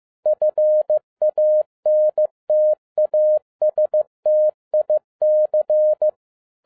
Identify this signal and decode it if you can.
Morse code, heard as a keyed tone, about 15 wpm, 610 Hz. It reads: FANTASTIC
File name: 2011-07-mp3 / n02 JF2PDA 15wpm